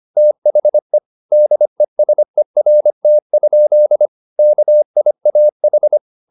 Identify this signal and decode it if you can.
Morse code, heard as a keyed tone, about 25 wpm, 600 Hz. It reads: THE DESERT? KIAH